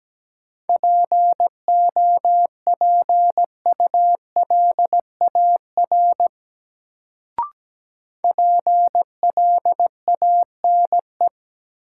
Morse code, heard as a keyed tone, about 17 wpm, 700 Hz. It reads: POPULAR PLANE
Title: Sentences from Top 1000 Words - Review Part 3 - New Word Drill - 17wpm